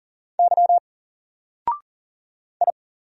Morse code, heard as a keyed tone, about 40 wpm, 700 Hz. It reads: Y I
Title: Single Letter-Number 40wpm